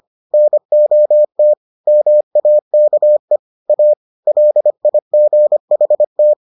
Morse code, heard as a keyed tone, about 25 wpm, 600 Hz. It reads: NOT MAKE A LIGHT